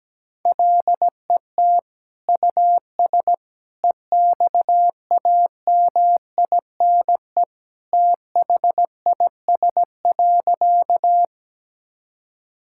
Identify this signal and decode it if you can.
Morse code, heard as a keyed tone, about 17 wpm, 700 Hz. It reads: LET US EXAMINE THIS.
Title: Sentences from Top 1600 Words 17wpm